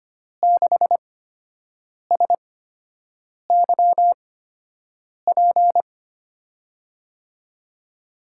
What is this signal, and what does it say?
Morse code, heard as a keyed tone, about 25 wpm, spaced as for 7 wpm, 700 Hz. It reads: 6SYP
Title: USA Call Signs - 8x Character Spacing - 25wpm